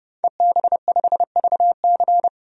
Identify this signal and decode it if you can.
Morse code, heard as a keyed tone, about 30 wpm, 700 Hz. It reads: EB5VC
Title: DXCC-100 Call Signs - 5 Characters - 30wpm